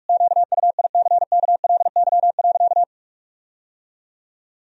Morse code, heard as a keyed tone, about 43 wpm, 700 Hz. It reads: QUICKLY.
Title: Sentences from Top 1000 Words - Review Part 4 - 45wpm